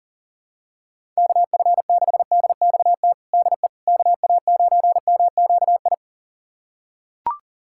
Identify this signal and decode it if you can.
Morse code, heard as a keyed tone, about 40 wpm, 700 Hz. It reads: KF6DXT DE KA9MQI
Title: USA Callsign DE Callsign - 6 Characters - 40wpm